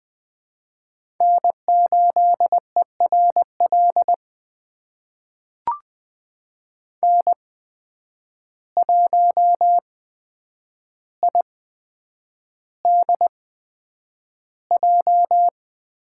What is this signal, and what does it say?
Morse code, heard as a keyed tone, about 20 wpm, spaced as for 9 wpm, 700 Hz. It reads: N8ERL N1IDJ
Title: USA Call Signs - 8x Character Spacing - 20wpm